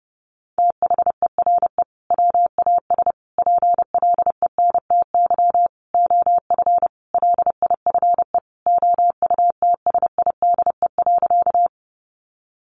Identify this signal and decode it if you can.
Morse code, heard as a keyed tone, about 30 wpm, 700 Hz. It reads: THERE WAS PLENTY OF LIFE OUTSIDE.
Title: Sentences from Top 1300 Words 30wpm